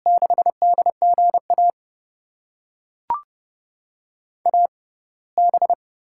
Morse code, heard as a keyed tone, about 32 wpm, spaced as for 21 wpm, 700 Hz. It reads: 6DGA AB
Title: USA Call Signs - 6x Character Spacing - 30wpm